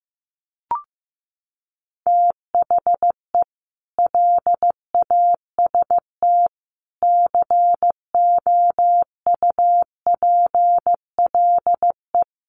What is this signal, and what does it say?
Morse code, heard as a keyed tone, about 15 wpm, 700 Hz. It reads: THE LAST COUPLE
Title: Sets of 5 Words 15wpm